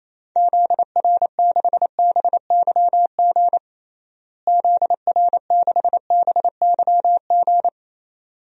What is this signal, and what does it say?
Morse code, heard as a keyed tone, about 28 wpm, 700 Hz. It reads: ZR6BYG ZR6BYG
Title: DXCC-100 Call Signs - 6 Characters - Repeated 2x - 28wpm